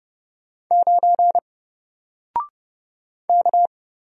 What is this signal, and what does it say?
Morse code, heard as a keyed tone, about 31 wpm, 700 Hz. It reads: 9 K
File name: Single-Letter-Number-30wpm